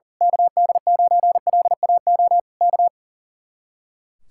Morse code, heard as a keyed tone, about 35 wpm, 700 Hz. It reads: KD9LAO K
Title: Speed-Racing Common QSO Phrases Encore - 40wpm